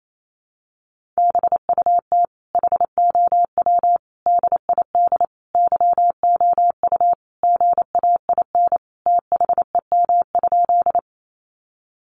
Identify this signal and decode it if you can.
Morse code, heard as a keyed tone, about 28 wpm, 700 Hz. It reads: BUT HOW DID YOU GAIN THEM?
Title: Sentences from Top 1000 Words 28wpm